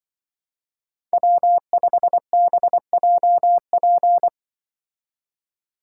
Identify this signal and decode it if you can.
Morse code, heard as a keyed tone, about 25 wpm, 700 Hz. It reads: W5BJP
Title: Speed Racing - USA 1x3 Call Signs 20wpm